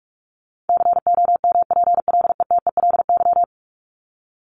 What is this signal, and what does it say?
Morse code, heard as a keyed tone, about 44 wpm, 700 Hz. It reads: COMPLETELY
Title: Top 1001-1250 Words 45wpm